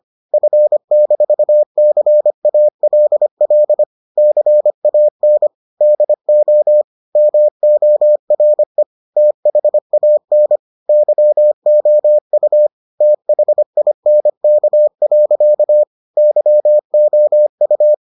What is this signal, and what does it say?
Morse code, heard as a keyed tone, about 25 wpm, 600 Hz. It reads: F-CALL CAN DO MORE THAN YOU THINK. YOU